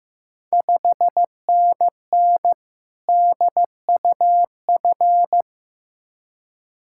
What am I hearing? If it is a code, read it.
Morse code, heard as a keyed tone, about 15 wpm, 700 Hz. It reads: 5NN DUF
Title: Ontario QSO Party Exchange 15wpm